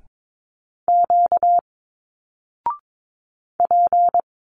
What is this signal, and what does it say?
Morse code, heard as a keyed tone, about 22 wpm, 700 Hz. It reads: Q P
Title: Single Letters 22wpm